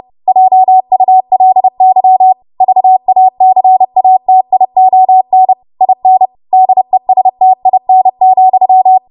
Morse code, heard as a keyed tone, about 30 wpm, 750 Hz. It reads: JULY VACATION IN DESTIN,
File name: Morse / 30WPM